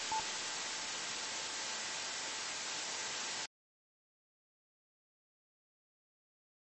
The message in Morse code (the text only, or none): none